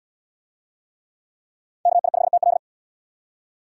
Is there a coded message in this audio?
Yes: Morse code from 1.9 to 2.6 s, heard as a keyed tone, about 67 wpm, 700 Hz. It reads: KI5TL